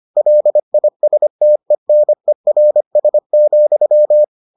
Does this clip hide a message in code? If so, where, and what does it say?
Yes: Morse code from 0.2 to 4.2 s, heard as a keyed tone, about 25 wpm, 600 Hz. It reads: LISTENERS,